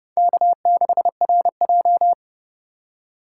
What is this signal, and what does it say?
Morse code, heard as a keyed tone, about 38 wpm, 700 Hz. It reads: K6RJ